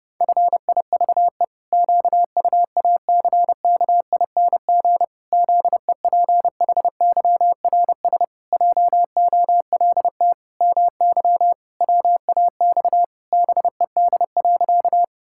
Morse code, heard as a keyed tone, about 29 wpm, 700 Hz. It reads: FIVE QUACKING ZEPHYRS JOLT MY WAX BED.
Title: Pangram Challenge v2 - 30wpm